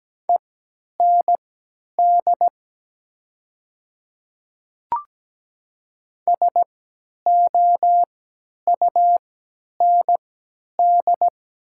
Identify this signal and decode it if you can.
Morse code, heard as a keyed tone, about 17 wpm, spaced as for 10 wpm, 700 Hz. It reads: END SOUND
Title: 3x Character Spacing - Top 300 Words - 17wpm